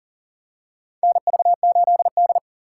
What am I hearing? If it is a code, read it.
Morse code, heard as a keyed tone, about 39 wpm, 700 Hz. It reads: NV8D
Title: USA Call Signs - 4 Characters - 4x Character Spacing - 40wpm